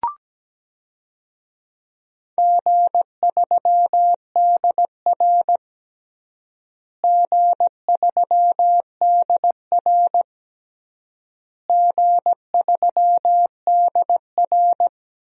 Morse code, heard as a keyed tone, about 17 wpm, 700 Hz. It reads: G3DR G3DR G3DR
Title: DXCC-100 Call Signs - 4 Characters - Repeated 3x - 17wpm